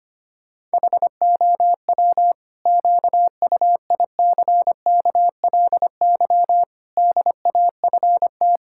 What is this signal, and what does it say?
Morse code, heard as a keyed tone, about 25 wpm, 700 Hz. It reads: HOW QUICKLY DAFT
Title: Pangram Challenge v2 - 25wpm